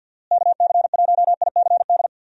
Morse code, heard as a keyed tone, about 51 wpm, 700 Hz. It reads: KC1ICD
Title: USA Call Signs - 6 Characters - Repeated 2x - 50wpm